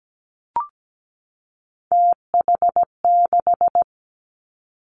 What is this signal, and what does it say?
Morse code, heard as a keyed tone, about 17 wpm, 700 Hz. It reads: TH6
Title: Three Challenging Letter-Number Combinations 17wpm